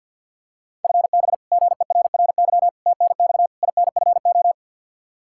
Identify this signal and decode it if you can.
Morse code, heard as a keyed tone, about 51 wpm, 700 Hz. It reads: FB GERRY TNX INFO
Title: Practice QSO - Exchange 3 - More Info - 50wpm